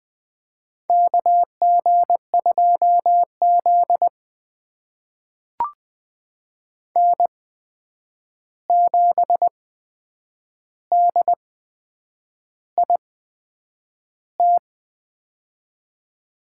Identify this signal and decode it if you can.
Morse code, heard as a keyed tone, about 20 wpm, spaced as for 8 wpm, 700 Hz. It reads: KG2Z N7DIT
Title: USA Call Signs - 8x Character Spacing - 20wpm